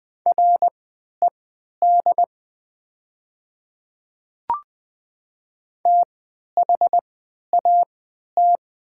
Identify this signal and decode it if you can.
Morse code, heard as a keyed tone, about 20 wpm, spaced as for 11 wpm, 700 Hz. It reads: RED THAT
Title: 3x Character Spacing - Top 500 Words - 20wpm